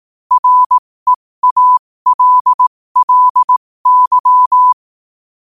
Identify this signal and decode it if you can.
Morse code, heard as a keyed tone, about 18 wpm, spaced as for 15 wpm, 1000 Hz. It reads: REALLY
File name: qotdfiles / 15wpm